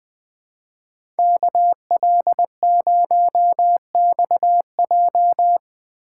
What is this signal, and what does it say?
Morse code, heard as a keyed tone, about 20 wpm, 700 Hz. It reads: KL0XJ